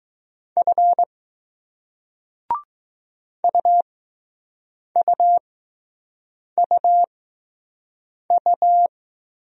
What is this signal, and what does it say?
Morse code, heard as a keyed tone, about 20 wpm, spaced as for 6 wpm, 700 Hz. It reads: F UUUU